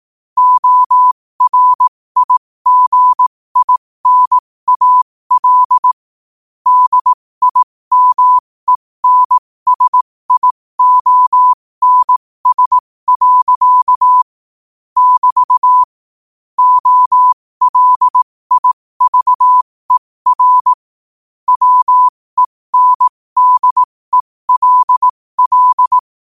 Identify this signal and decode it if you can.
Morse code, heard as a keyed tone, about 18 wpm, spaced as for 15 wpm, 1000 Hz. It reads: ORIGINAL DIMENSIONS. <BT> OLIVER WENDELL
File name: qotdfiles / 15wpm